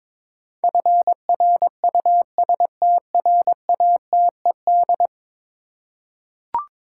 Morse code, heard as a keyed tone, about 22 wpm, 700 Hz. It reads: FRUSTRATED
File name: practice / 22wpm / 10 Letter Words - First 3 Letters Given - 22wpm